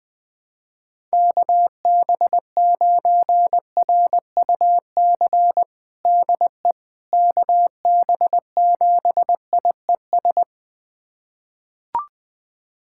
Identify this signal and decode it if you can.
Morse code, heard as a keyed tone, about 20 wpm, 700 Hz. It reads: KB9RUC DE KB7IES
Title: USA Callsign DE Callsign - 6 Characters - 20wpm